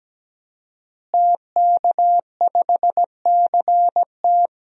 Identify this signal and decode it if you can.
Morse code, heard as a keyed tone, about 17 wpm, 700 Hz. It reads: TK5CT
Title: DXCC-100 Call Signs - 5 Characters - Repeated 2x - 17wpm